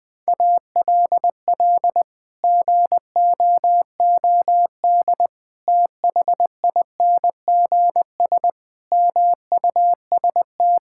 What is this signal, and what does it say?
Morse code, heard as a keyed tone, about 20 wpm, 700 Hz. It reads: ALL GOOD THINGS MUST
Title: Sets of 8 Words - 20wpm